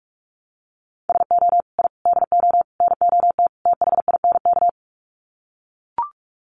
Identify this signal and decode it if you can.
Morse code, heard as a keyed tone, about 44 wpm, 700 Hz. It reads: SO I DO NOT THINK
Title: Sets of 5 Words 45wpm